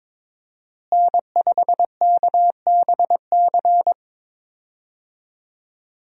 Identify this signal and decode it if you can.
Morse code, heard as a keyed tone, about 22 wpm, 700 Hz. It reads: N5KBC
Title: USA Call Signs - 5 Characters - Repeated 3x - 22wpm